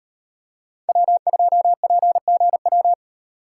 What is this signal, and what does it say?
Morse code, heard as a keyed tone, about 38 wpm, 700 Hz. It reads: W2PGW